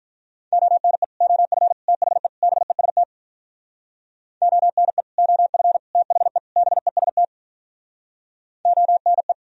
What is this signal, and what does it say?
Morse code, heard as a keyed tone, about 45 wpm, 700 Hz. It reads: ONE OF THE BEST ONE OF THE BEST ONE